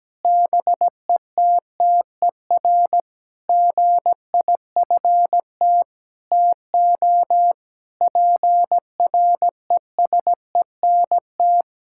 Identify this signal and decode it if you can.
Morse code, heard as a keyed tone, about 17 wpm, 700 Hz. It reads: BETTER GIFT TO PRESENT
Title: Sentences from Top 1000 Words - Review Part 6 - 17wpm